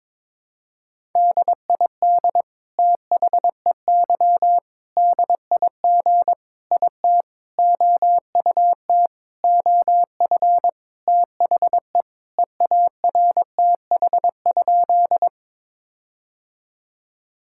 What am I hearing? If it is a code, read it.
Morse code, heard as a keyed tone, about 22 wpm, 700 Hz. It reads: DID THEY DIG IT OUT OF THE EARTH?
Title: Sentences from Top 1600 Words 22wpm